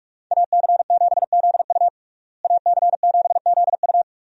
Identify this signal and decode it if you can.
Morse code, heard as a keyed tone, about 46 wpm, 700 Hz. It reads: AC7ZU AC7ZU
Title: USA Call Signs - 5 Characters - Repeated 3x - 45wpm